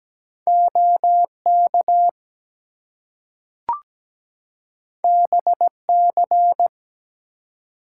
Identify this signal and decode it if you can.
Morse code, heard as a keyed tone, about 17 wpm, 700 Hz. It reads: OK BC
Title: US States and Canadian Provinces 17wpm